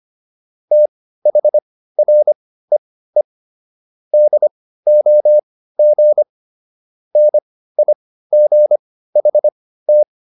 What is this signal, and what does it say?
Morse code, heard as a keyed tone, about 25 wpm, spaced as for 15 wpm, 600 Hz. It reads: THREE DOG NIGHT